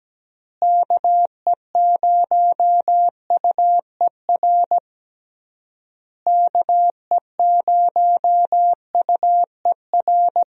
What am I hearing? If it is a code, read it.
Morse code, heard as a keyed tone, about 17 wpm, 700 Hz. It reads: KE0UER KE0UER
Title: USA Call Signs - 6 Characters - Repeated 3x - 17wpm